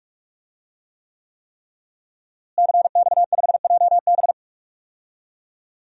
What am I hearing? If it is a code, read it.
Morse code, heard as a keyed tone, about 48 wpm, 700 Hz. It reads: CX5JB
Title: DXCC-100 Call Signs - 5 Characters - Repeated 3x - 45wpm